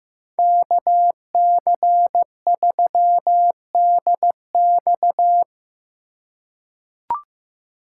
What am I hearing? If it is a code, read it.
Morse code, heard as a keyed tone, about 15 wpm, 700 Hz. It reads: KC3DX
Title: USA Call Signs - 5 Characters - 2x Character Spacing - 15wpm